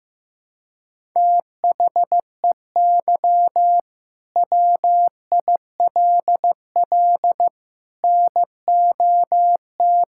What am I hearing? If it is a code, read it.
Morse code, heard as a keyed tone, about 15 wpm, 700 Hz. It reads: THEY WILL NOT
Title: Sentences from Top 1000 Words - Review Part 5 - 15wpm